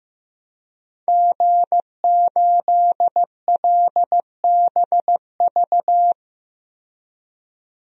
Morse code, heard as a keyed tone, about 15 wpm, 700 Hz. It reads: G8LBV
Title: DXCC-100 Call Signs - 5 Characters - 15wpm